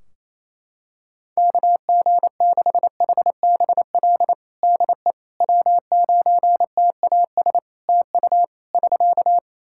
Morse code, heard as a keyed tone, about 28 wpm, 700 Hz. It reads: KG6HBL DE W9TAS TU <SK>
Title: Practice QSO - Exchange 4 - Ending - 28wpm